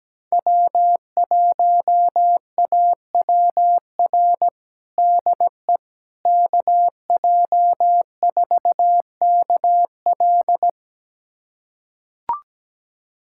Morse code, heard as a keyed tone, about 17 wpm, 700 Hz. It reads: W1AWR DE KJ4KL
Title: USA Callsign DE Callsign - 5 Characters - 17wpm